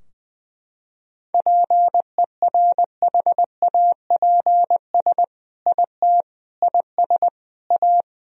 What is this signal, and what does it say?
Morse code, heard as a keyed tone, about 20 wpm, 700 Hz. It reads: PERHAPS IT IS A
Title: Sentences from Top 1000 Words - Review Part 3 - 20wpm